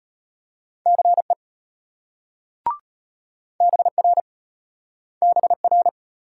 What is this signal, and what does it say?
Morse code, heard as a keyed tone, about 34 wpm, 700 Hz. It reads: CE BR BR